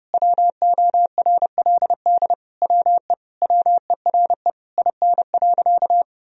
Morse code, heard as a keyed tone, about 30 wpm, 700 Hz. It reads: WORLD WE WERE IN.
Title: Sentences from Top 1000 Words - Review Part 3 - 30wpm